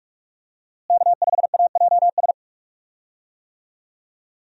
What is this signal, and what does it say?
Morse code, heard as a keyed tone, about 44 wpm, 700 Hz. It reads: K5AJS
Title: USA Call Signs - 5 Characters - Repeated 3x - 45wpm